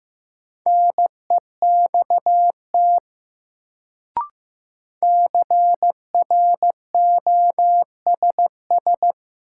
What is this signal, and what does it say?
Morse code, heard as a keyed tone, about 15 wpm, 700 Hz. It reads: NEXT CROSS
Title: Sentences from Top 300 Words - New Word Drill - 15wpm